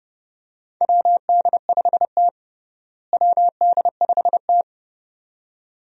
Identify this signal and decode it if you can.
Morse code, heard as a keyed tone, about 30 wpm, 700 Hz. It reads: WD5T WD5T
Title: USA Call Signs - 4 Characters - Repeated 2x - 30wpm